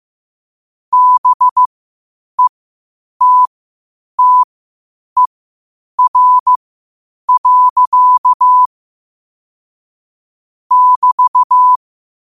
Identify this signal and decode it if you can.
Morse code, heard as a keyed tone, about 15 wpm, spaced as for 8 wpm, 1000 Hz. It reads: BETTER. <BT>
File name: qotdfiles / 8wpm